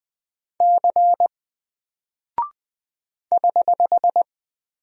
Morse code, heard as a keyed tone, about 20 wpm, 700 Hz. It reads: C <HH>
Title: Single Character (Letter:Number:Punctuation:Prosign) 20wpm